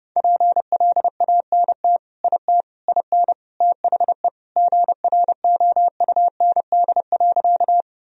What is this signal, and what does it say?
Morse code, heard as a keyed tone, about 30 wpm, 700 Hz. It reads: PLANT IT IN THE GROUND.
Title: Sentences from Top 1000 Words - Review Part 1 - 30wpm